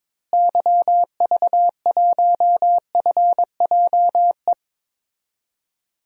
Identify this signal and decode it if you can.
Morse code, heard as a keyed tone, about 21 wpm, 700 Hz. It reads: YV1FJE
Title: DXCC-100 Call Signs - 6 Characters - Repeated 3x - 22wpm